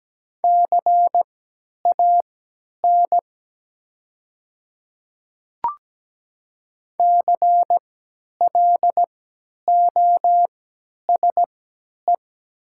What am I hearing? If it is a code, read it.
Morse code, heard as a keyed tone, about 17 wpm, spaced as for 10 wpm, 700 Hz. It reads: CAN CLOSE
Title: 3x Character Spacing - Top 300 Words - 17wpm